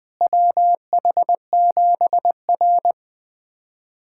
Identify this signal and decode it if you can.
Morse code, heard as a keyed tone, about 20 wpm, 700 Hz. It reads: WH7R
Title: USA Callsign DE Callsign - 4 Characters - 20wpm